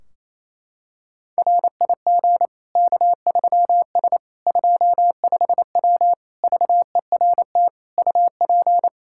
Morse code, heard as a keyed tone, about 28 wpm, 700 Hz. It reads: RIG K3S 25W VERT UP
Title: Practice QSO - Exchange 2 - Info - 28wpm